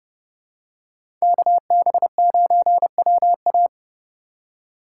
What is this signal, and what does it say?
Morse code, heard as a keyed tone, about 30 wpm, 700 Hz. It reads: KB9WA